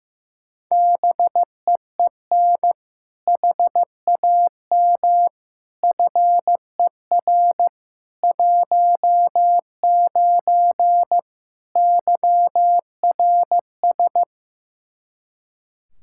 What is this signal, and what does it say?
Morse code, heard as a keyed tone, about 15 wpm, 700 Hz. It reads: BEEN HAM FER 19 YRS
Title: Practice QSO - Exchange 3 - More Info - 15wpm